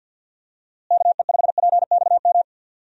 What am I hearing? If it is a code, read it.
Morse code, heard as a keyed tone, about 52 wpm, 700 Hz. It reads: KE5PXM
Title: USA Call Signs - 6 Characters - Repeated 3x - 50wpm